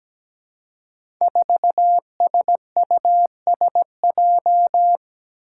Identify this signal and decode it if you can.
Morse code, heard as a keyed tone, about 17 wpm, 700 Hz. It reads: 4SUSJ
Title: Five Challenging Letter-Number Combinations 17wpm